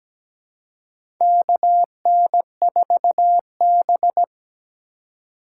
Morse code, heard as a keyed tone, about 17 wpm, 700 Hz. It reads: KN4B
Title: USA Call Signs - 4 Characters - 3x Character Spacing - 17wpm